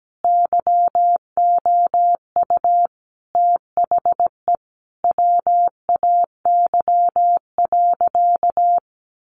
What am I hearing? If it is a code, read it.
Morse code, heard as a keyed tone, about 17 wpm, 700 Hz. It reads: YOU THE WAY.
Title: Sentences from Top 300 Words 17wpm